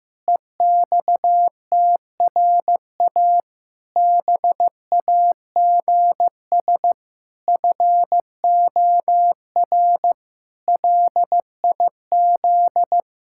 Morse code, heard as a keyed tone, about 15 wpm, 700 Hz. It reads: EXTRA BAGS FOR LIZ
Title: Pangram Challenge v2 - 15wpm